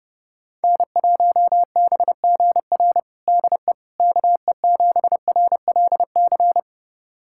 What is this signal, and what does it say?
Morse code, heard as a keyed tone, about 32 wpm, 700 Hz. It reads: N1BGR DE KE7RLC
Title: USA Callsign DE Callsign 30wpm